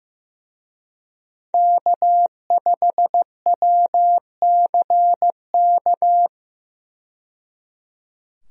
Morse code, heard as a keyed tone, about 15 wpm, 700 Hz. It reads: K5WCK
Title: USA Call Signs - 5 Characters - 15wpm